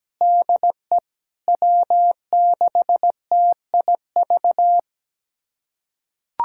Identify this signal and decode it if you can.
Morse code, heard as a keyed tone, about 17 wpm, 700 Hz. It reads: DE W6TIV
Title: USA Callsign DE Callsign 17wpm